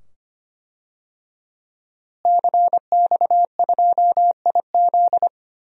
Morse code, heard as a keyed tone, about 25 wpm, 700 Hz. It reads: CX2IZ
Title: DXCC-100 Call Signs - 5 Characters - Repeated 2x - 25wpm